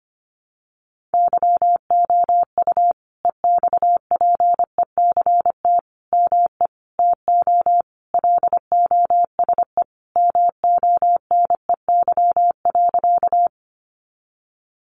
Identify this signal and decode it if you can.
Morse code, heard as a keyed tone, about 25 wpm, 700 Hz. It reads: YOU EXPECT ME TO LOSE MONEY.
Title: Sentences from Top 1100 Words 25wpm